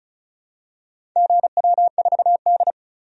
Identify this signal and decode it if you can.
Morse code, heard as a keyed tone, about 34 wpm, 700 Hz. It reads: GW4D
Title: DXCC-100 Call Signs - 4 Characters - 35wpm